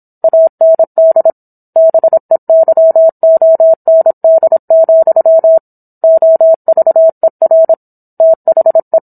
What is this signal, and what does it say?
Morse code, heard as a keyed tone, about 27 wpm, 650 Hz. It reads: AND BEYOND, OVER THE